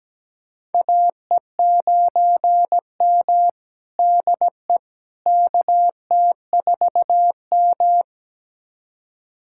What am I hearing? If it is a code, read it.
Morse code, heard as a keyed tone, about 17 wpm, 700 Hz. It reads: AE9M DE KT4M
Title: USA Callsign DE Callsign - 4 Characters - 17wpm